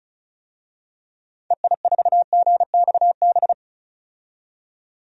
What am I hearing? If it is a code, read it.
Morse code, heard as a keyed tone, about 34 wpm, 700 Hz. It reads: EI4GXB
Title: DXCC-100 Call Signs - 6 Characters - 35wpm